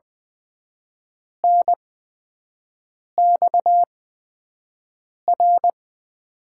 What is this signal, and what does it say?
Morse code, heard as a keyed tone, about 20 wpm, spaced as for 5 wpm, 700 Hz. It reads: NXR
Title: USA Call Signs - 5 Characters - 8x Character Spacing - 20wpm